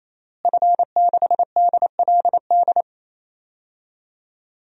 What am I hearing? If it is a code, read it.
Morse code, heard as a keyed tone, about 27 wpm, 700 Hz. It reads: F6DLD
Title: DXCC-100 Call Signs - 5 Characters - Repeated 3x - 28wpm